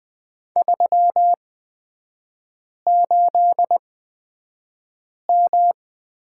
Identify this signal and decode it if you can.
Morse code, heard as a keyed tone, about 20 wpm, spaced as for 5 wpm, 700 Hz. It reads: 38M
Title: 224 Lesson 36e - BT - Three Characters 20wpm